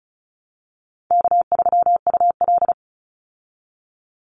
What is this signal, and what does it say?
Morse code, heard as a keyed tone, about 34 wpm, 700 Hz. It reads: K3UL
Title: Easy Call Signs Repeated 2x - 35wpm